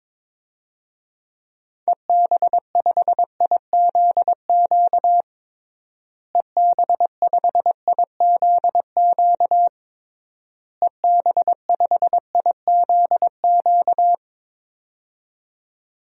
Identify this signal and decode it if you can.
Morse code, heard as a keyed tone, about 21 wpm, 700 Hz. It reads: EB5IZQ EB5IZQ EB5IZQ
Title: DXCC-100 Call Signs - 6 Characters - Repeated 3x - 22wpm